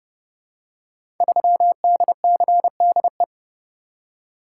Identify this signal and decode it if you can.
Morse code, heard as a keyed tone, about 30 wpm, 700 Hz. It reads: 3DCDE
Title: Code Groups - 5 Letter:Numbers - 30wpm